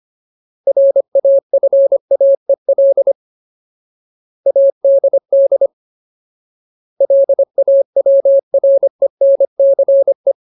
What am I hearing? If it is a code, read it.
Morse code, heard as a keyed tone, about 25 wpm, 550 Hz. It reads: RAFAEL ADD LAWRENCE
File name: Ultimate Word List / 25WPM Ultimate